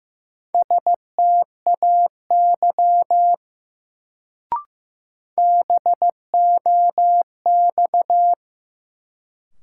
Morse code, heard as a keyed tone, about 15 wpm, 700 Hz. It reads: STAY BOX
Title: Sentences from Top 500 Words - New Word Drill - 15wpm